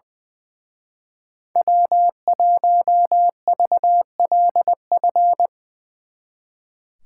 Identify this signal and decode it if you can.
Morse code, heard as a keyed tone, about 24 wpm, 700 Hz. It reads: W1VLF
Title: Speed Racing - USA Call Signs 20wpm